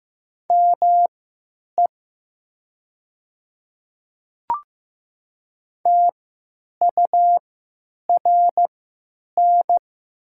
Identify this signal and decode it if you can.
Morse code, heard as a keyed tone, about 15 wpm, spaced as for 8 wpm, 700 Hz. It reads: ME TURN
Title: 3x Character Spacing - Top 200 Words - 15wpm